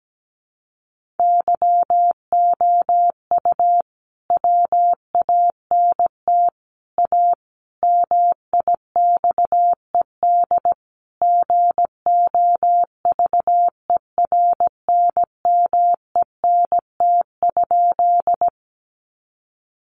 Morse code, heard as a keyed tone, about 17 wpm, 700 Hz. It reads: YOU WANT A MIXED GOVERNMENT?
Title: Sentences from Top 1500 Words 17wpm